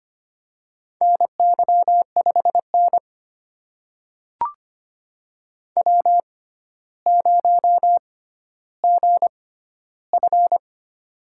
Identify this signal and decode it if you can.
Morse code, heard as a keyed tone, about 25 wpm, spaced as for 13 wpm, 700 Hz. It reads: NY5N W0GF
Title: USA Call Signs - 4 Characters - 6x Character Spacing - 25wpm